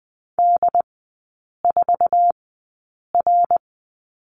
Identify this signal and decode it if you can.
Morse code, heard as a keyed tone, about 20 wpm, spaced as for 8 wpm, 700 Hz. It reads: D4R
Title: 093 Lesson 16e - Letter c - Three Characters 20wpm